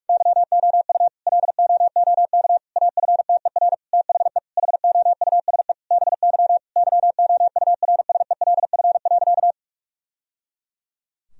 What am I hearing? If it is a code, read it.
Morse code, heard as a keyed tone, about 46 wpm, 700 Hz. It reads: YOU LOOK AFTER THE HOUSE BY YOURSELF.
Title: Sentences from Top 1000 Words - Review Part 4 - 45wpm